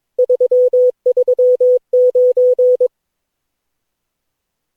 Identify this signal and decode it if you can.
Morse code, heard as a keyed tone, about 22 wpm, 500 Hz. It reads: 339
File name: cw / Session9 cw training